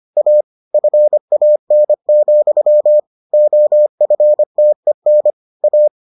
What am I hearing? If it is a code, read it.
Morse code, heard as a keyed tone, about 25 wpm, 600 Hz. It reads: A FAN, OFTEN A